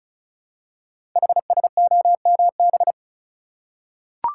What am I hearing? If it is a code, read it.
Morse code, heard as a keyed tone, about 36 wpm, 700 Hz. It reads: HSOMB
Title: Code Groups - 5 Letters - 35wpm